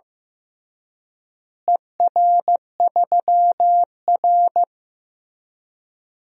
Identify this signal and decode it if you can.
Morse code, heard as a keyed tone, about 15 wpm, 700 Hz. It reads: ER3R
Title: DXCC-100 Call Signs - 4 Characters - Repeated 3x - 15wpm